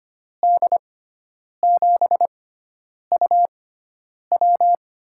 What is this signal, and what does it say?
Morse code, heard as a keyed tone, about 25 wpm, spaced as for 9 wpm, 700 Hz. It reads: D7UW
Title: USA Call Signs - 5 Characters - 6x Character Spacing - 25wpm